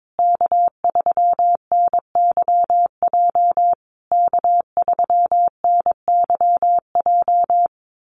Morse code, heard as a keyed tone, about 22 wpm, 700 Hz. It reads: K3NYJ K3NYJ
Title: USA Call Signs Repeated 3x 22wpm